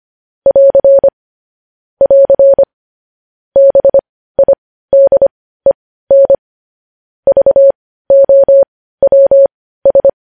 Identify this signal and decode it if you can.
Morse code, heard as a keyed tone, about 25 wpm, spaced as for 15 wpm, 550 Hz. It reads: <AR> <AR> BIDEN VOWS
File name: podcast / News headlines-1x-25x15